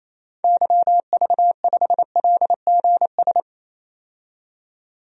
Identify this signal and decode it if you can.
Morse code, heard as a keyed tone, about 28 wpm, 700 Hz. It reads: YV5LGS
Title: DXCC-100 Call Signs - 6 Characters - 28wpm